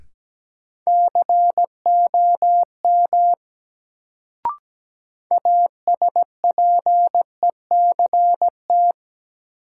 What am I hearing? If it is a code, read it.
Morse code, heard as a keyed tone, about 17 wpm, 700 Hz. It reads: COM ASPECT
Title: Sentences from Top 1300 Words - New Word Drill Alt - 17wpm